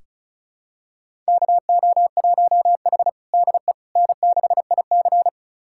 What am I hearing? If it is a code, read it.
Morse code, heard as a keyed tone, about 34 wpm, 700 Hz. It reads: KO1H DE N6IC
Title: USA Callsign DE Callsign - 4 Characters - 35wpm